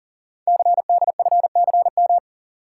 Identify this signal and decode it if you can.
Morse code, heard as a keyed tone, about 38 wpm, 690 Hz. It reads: CDFCM
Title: Code Groups - 5 Letters - 40wpm